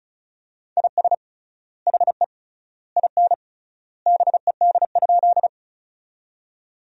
Morse code, heard as a keyed tone, about 36 wpm, 700 Hz. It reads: IS HE IN BED?
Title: 3x Word-Spacing - Sentences from Top 500 Words - 35wpm